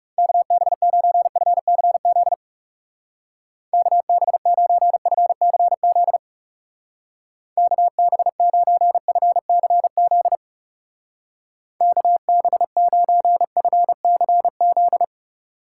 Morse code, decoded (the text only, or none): KB9FCZ KB9FCZ KB9FCZ KB9FCZ